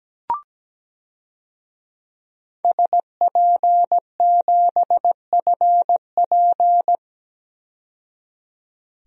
Morse code, heard as a keyed tone, about 17 wpm, 700 Hz. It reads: SP7FP
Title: DXCC-100 Call Signs - 5 Characters - 17wpm